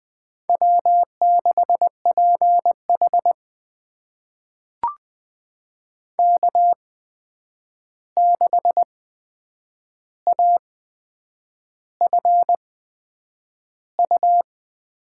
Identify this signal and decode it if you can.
Morse code, heard as a keyed tone, about 20 wpm, spaced as for 8 wpm, 700 Hz. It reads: W6PH K6AFU
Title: USA Call Signs - 8x Character Spacing - 20wpm